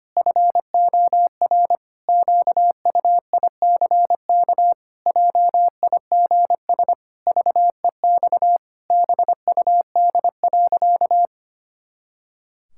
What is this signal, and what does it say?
Morse code, heard as a keyed tone, about 25 wpm, 700 Hz. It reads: FOR QUICK JIGS VEX BUD.
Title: Pangram Challenge v2 - 25wpm